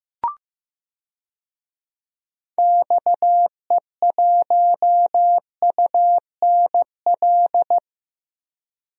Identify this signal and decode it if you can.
Morse code, heard as a keyed tone, about 15 wpm, 700 Hz. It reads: XE1UNL
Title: DXCC-100 Call Signs - 6 Characters - Repeated 3x - 15wpm